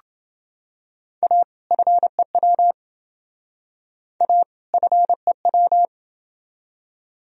Morse code, heard as a keyed tone, about 28 wpm, 700 Hz. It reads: A FEW A FEW